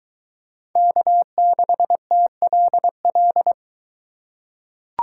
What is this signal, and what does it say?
Morse code, heard as a keyed tone, about 22 wpm, 700 Hz. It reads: K6TLL